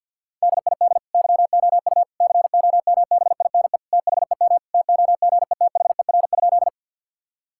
Morse code, heard as a keyed tone, about 50 wpm, 700 Hz. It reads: DID YOU COMBINE THEM TOGETHER?
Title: Sentences from Top 1600 Words 50wpm